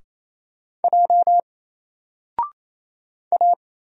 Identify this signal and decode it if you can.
Morse code, heard as a keyed tone, about 27 wpm, 700 Hz. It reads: J A
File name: Single-Letter-Number-28wpm